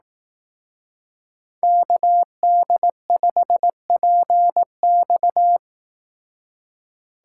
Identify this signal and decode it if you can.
Morse code, heard as a keyed tone, about 18 wpm, 700 Hz. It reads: KD5PX
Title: Speed Racing - USA 2x2 Call Signs 15wpm